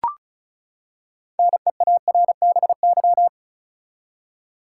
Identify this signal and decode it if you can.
Morse code, heard as a keyed tone, about 36 wpm, 700 Hz. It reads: NEARBY
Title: Six Letter Words - 35wpm